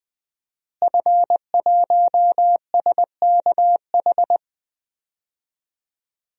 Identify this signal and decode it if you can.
Morse code, heard as a keyed tone, about 20 wpm, 700 Hz. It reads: F1SKH
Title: DXCC-100 Call Signs - 5 Characters - Repeated 3x - 20wpm